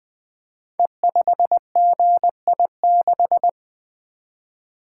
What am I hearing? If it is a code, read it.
Morse code, heard as a keyed tone, about 20 wpm, 700 Hz. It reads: E5GI6
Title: Code Groups - 5 Letter:Numbers - 20wpm